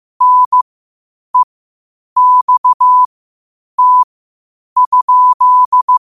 Morse code, heard as a keyed tone, about 15 wpm, spaced as for 8 wpm, 1000 Hz. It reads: NEXT?
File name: qotdfiles / 8wpm